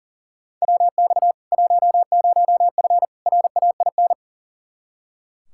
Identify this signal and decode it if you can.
Morse code, heard as a keyed tone, about 39 wpm, 700 Hz. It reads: WX 10F RAIN
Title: K1USN SST QSO Practice 40wpm